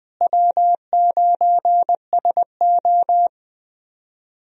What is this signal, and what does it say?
Morse code, heard as a keyed tone, about 20 wpm, 700 Hz. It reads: W9SO
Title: USA Call Signs - 4 Characters - Repeated 3x - 20wpm